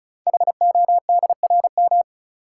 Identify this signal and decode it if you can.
Morse code, heard as a keyed tone, about 36 wpm, 700 Hz. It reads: HODRM
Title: Code Groups - 5 Letters - 35wpm